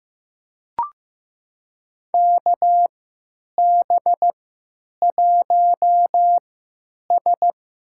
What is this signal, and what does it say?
Morse code, heard as a keyed tone, about 16 wpm, spaced as for 9 wpm, 700 Hz. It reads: KB1S
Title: USA Call Signs - 3x Character Spacing - 15wpm